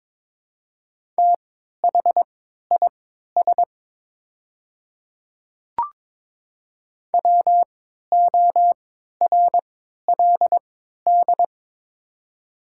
Morse code, heard as a keyed tone, about 22 wpm, spaced as for 12 wpm, 700 Hz. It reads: THIS WORLD
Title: 3x Character Spacing - Top 300 Words - 22wpm